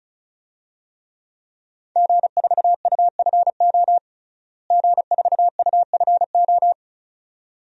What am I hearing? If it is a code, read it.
Morse code, heard as a keyed tone, about 36 wpm, 700 Hz. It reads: G4UFO G4UFO
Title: DXCC-100 Call Signs - 5 Characters - Repeated 2x - 35wpm